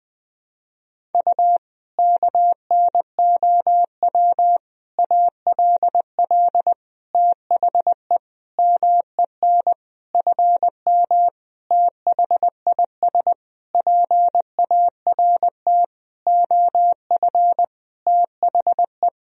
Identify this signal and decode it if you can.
Morse code, heard as a keyed tone, about 20 wpm, 700 Hz. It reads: U KNOW ALL THE MEN FM THIS PART OF THE
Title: Sentences from Top 300 Words and Abbreviations 20wpm